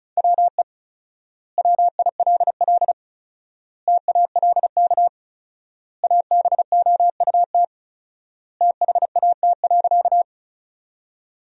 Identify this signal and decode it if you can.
Morse code, heard as a keyed tone, about 36 wpm, 700 Hz. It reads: WE WILL TALK ABOUT THAT.
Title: 4x Word-Spacing - Sentences from Top 400 Words - 35wpm